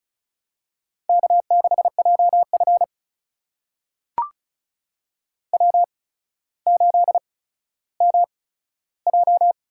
Morse code, heard as a keyed tone, about 36 wpm, spaced as for 15 wpm, 700 Hz. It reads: K6JF W8MJ
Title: USA Call Signs - 4 Characters - 8x Character Spacing - 35wpm